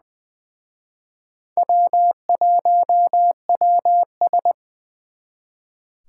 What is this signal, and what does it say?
Morse code, heard as a keyed tone, about 25 wpm, 700 Hz. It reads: W1WS
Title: Speed Racing - USA 1x2 Call Signs 20wpm